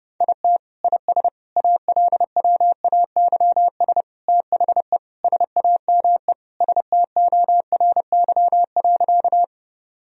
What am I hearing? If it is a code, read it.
Morse code, heard as a keyed tone, about 32 wpm, 700 Hz. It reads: IT IS ALWAYS THE SAME STORY.
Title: Sentences from Top 1000 Words - Review Part 1 - 30wpm